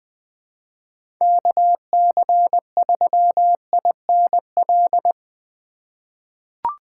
Morse code, heard as a keyed tone, about 20 wpm, 700 Hz. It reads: KC3INL
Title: USA Call Signs - 6 Characters - 4x Character Spacing - 20wpm